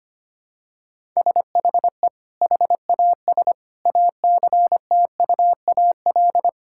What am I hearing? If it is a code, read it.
Morse code, heard as a keyed tone, about 25 wpm, 700 Hz. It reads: SHE HAS ACTUAL
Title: Sentences from Top 1000 Words - Review Part 5 - 25wpm